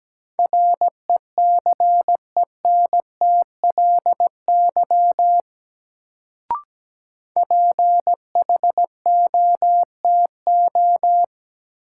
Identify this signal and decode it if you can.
Morse code, heard as a keyed tone, about 17 wpm, 700 Hz. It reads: RECENTLY PHOTO
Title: Sentences from Top 1200 Words - New Word Drill Alt - 17wpm